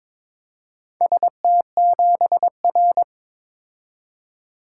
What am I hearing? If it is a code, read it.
Morse code, heard as a keyed tone, about 22 wpm, 700 Hz. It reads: ST7R
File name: practice / 22wpm / Four Letter-Number Combinations - 22wpm